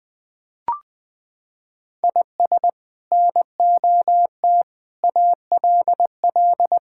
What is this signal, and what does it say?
Morse code, heard as a keyed tone, about 20 wpm, 700 Hz. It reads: IS NOT ALL
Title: Sentences from Top 1600 Words 20wpm